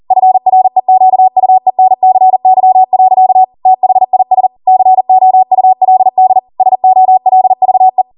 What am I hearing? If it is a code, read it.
Morse code, heard as a keyed tone, about 40 wpm, 750 Hz. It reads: FREQUENCY. THIS COULD SOLVE
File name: Morse / 40WPM